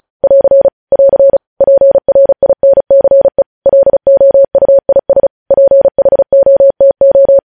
Morse code, heard as a keyed tone, about 36 wpm, 550 Hz. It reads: <AR> <AR> PRINCE LOUIS PHOTO
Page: News Headlines in Morse Code at 35 WPM